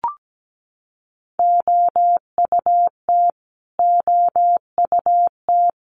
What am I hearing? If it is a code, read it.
Morse code, heard as a keyed tone, about 17 wpm, 700 Hz. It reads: OUT OUT
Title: Top 300 Words Repeated 17wpm